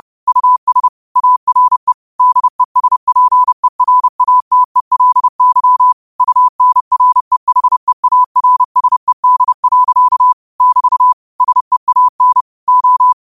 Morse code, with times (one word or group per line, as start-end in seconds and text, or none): 0.3-0.9 US
1.2-1.9 ARE
2.2-5.9 DESPERATELY
6.2-10.3 UNREHEARSED.
10.6-11.1 <BT>
11.4-12.4 SEAN
12.7-13.1 O